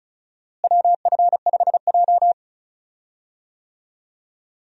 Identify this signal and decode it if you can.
Morse code, heard as a keyed tone, about 36 wpm, 700 Hz. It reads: WF5J
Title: USA Call Signs - 4 Characters - Repeated 2x - 35wpm